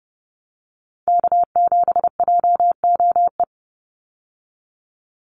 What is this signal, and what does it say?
Morse code, heard as a keyed tone, about 30 wpm, 700 Hz. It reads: K7JOE
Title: Call Signs - 30wpm